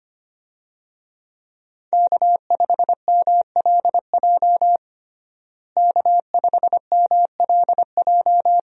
Morse code, heard as a keyed tone, about 25 wpm, 700 Hz. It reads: K5MLJ K5MLJ
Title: USA Call Signs - 5 Characters - Repeated 2x - 25wpm